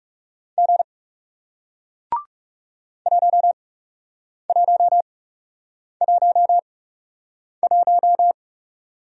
Morse code, heard as a keyed tone, about 38 wpm, spaced as for 8 wpm, 700 Hz. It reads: G 1111